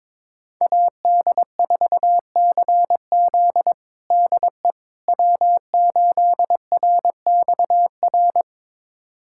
Morse code, heard as a keyed tone, about 22 wpm, 700 Hz. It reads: AD4CZ DE W8RXR
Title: USA Callsign DE Callsign - 5 Characters - 22wpm